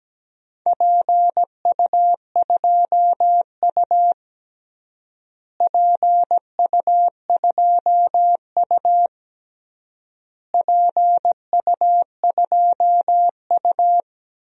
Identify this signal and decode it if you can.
Morse code, heard as a keyed tone, about 17 wpm, 700 Hz. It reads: PU2U PU2U PU2U
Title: DXCC-100 Call Signs - 4 Characters - Repeated 3x - 17wpm